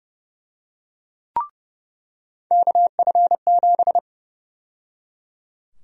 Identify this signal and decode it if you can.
Morse code, heard as a keyed tone, about 30 wpm, 700 Hz. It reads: KF7
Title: USA Call Sign Prefixes - 30wpm